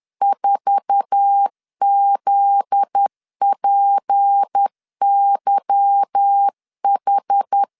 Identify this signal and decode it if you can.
Morse code, heard as a keyed tone, about 10 wpm, 800 Hz. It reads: 4ZPYH
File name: LEZIONI / 8 YJ45 / 8 YJ4 10wpm 2633